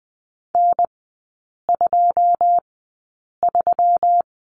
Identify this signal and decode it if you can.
Morse code, heard as a keyed tone, about 20 wpm, spaced as for 8 wpm, 690 Hz. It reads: N23
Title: 133 Lesson 22e - Question Mark - Three Characters 20wpm